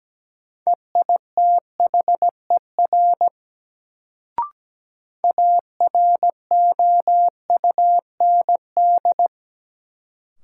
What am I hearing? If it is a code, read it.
Morse code, heard as a keyed tone, about 17 wpm, 700 Hz. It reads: EITHER AROUND
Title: Sentences from Top 600 Words - New Word Drill - 17wpm